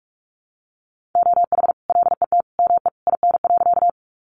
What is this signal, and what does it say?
Morse code, heard as a keyed tone, about 44 wpm, 700 Hz. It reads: OH LET ME IN.
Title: Sentences from Top 500 Words 45wpm